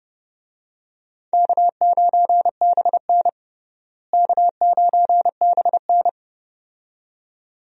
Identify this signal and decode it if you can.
Morse code, heard as a keyed tone, about 32 wpm, 700 Hz. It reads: K9BN K9BN
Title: USA Call Signs - 4 Characters - Repeated 2x - 30wpm